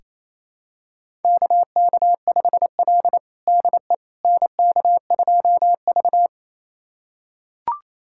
Morse code, heard as a keyed tone, about 27 wpm, 700 Hz. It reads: KK5L DE NK2V
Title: USA Callsign DE Callsign - 4 Characters - 28wpm